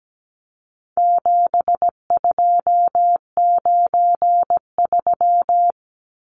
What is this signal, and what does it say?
Morse code, heard as a keyed tone, about 17 wpm, 700 Hz. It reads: 7293
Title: Numbers - 4-Digit 17wpm